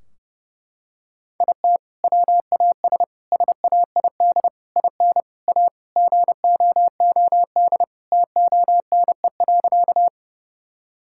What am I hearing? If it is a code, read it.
Morse code, heard as a keyed tone, about 30 wpm, 700 Hz. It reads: IT WAS SAID IN A GOOD TONE.
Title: Sentences from Top 1600 Words 30wpm